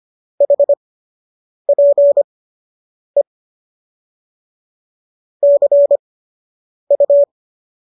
Morse code, heard as a keyed tone, about 25 wpm, spaced as for 8 wpm, 570 Hz. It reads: HPE CU